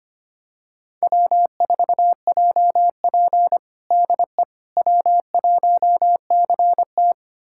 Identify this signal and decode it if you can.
Morse code, heard as a keyed tone, about 25 wpm, 700 Hz. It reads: W4JP DE W1CT
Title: USA Callsign DE Callsign - 4 Characters - 25wpm